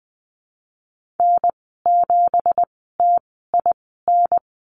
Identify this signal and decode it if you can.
Morse code, heard as a keyed tone, about 20 wpm, spaced as for 15 wpm, 700 Hz. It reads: N7TIN
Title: 173 Lesson 28g - Number 9 - Call Signs 20wpm